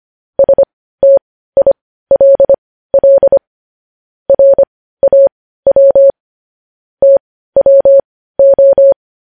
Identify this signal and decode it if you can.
Morse code, heard as a keyed tone, about 25 wpm, spaced as for 15 wpm, 550 Hz. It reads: STILL RAW TWO